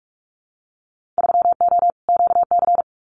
Morse code, heard as a keyed tone, about 46 wpm, 700 Hz. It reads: 3O QC
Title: Winter Field Day Exchanges Encore 45wpm